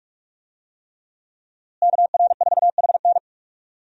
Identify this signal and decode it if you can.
Morse code, heard as a keyed tone, about 46 wpm, 700 Hz. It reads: KR4HN
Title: USA Call Signs - 5 Characters - Repeated 3x - 45wpm